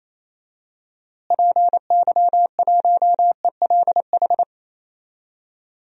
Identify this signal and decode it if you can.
Morse code, heard as a keyed tone, about 28 wpm, 700 Hz. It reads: PY1ELH
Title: DXCC-100 Call Signs - 6 Characters - Repeated 3x - 28wpm